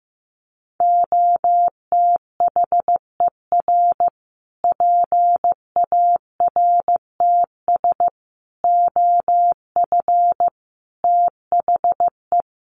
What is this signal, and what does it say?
Morse code, heard as a keyed tone, about 15 wpm, 700 Hz. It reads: OTHER PARTS OF THE
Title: Sets of 5 Words Encore 15wpm